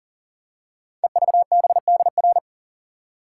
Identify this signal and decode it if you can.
Morse code, heard as a keyed tone, about 42 wpm, 700 Hz. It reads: EVBDR
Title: Code Groups - 5 Letter:Numbers - 40wpm